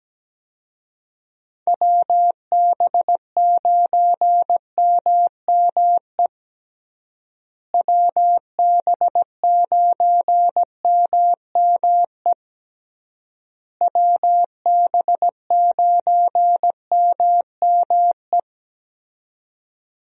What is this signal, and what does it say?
Morse code, heard as a keyed tone, about 17 wpm, 700 Hz. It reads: WB9MME WB9MME WB9MME
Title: USA Call Signs - 6 Characters - Repeated 3x - 17wpm